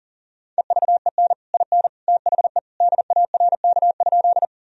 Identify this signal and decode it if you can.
Morse code, heard as a keyed tone, about 40 wpm, 700 Hz. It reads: EVEN IN THE DARK?
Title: Sentences from Top 1000 Words - Review Part 1 - 40wpm